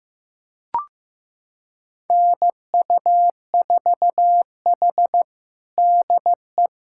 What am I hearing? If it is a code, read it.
Morse code, heard as a keyed tone, about 15 wpm, 700 Hz. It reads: NU4H DE
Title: USA Callsign DE Callsign - 4 Characters - 15wpm